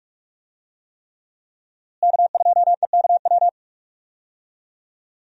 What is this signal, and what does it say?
Morse code, heard as a keyed tone, about 46 wpm, 700 Hz. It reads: K2EKW
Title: USA Call Signs - 5 Characters - Repeated 3x - 45wpm